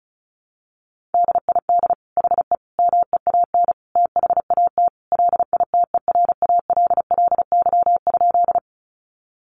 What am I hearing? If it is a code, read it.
Morse code, heard as a keyed tone, about 34 wpm, 700 Hz. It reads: DID HE MEAN THAT LITERALLY?
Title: Sentences from Top 1400 Words 35wpm